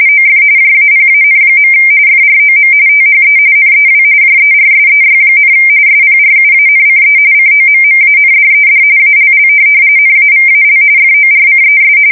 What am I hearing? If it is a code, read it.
RTTY, about 45.45 baud. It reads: OF JAPAN. FROM THE NEW JERSEY, SPRUANCE COMMANDED ALL NAVAL FORCES
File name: fsk170 / USS-New-Jersey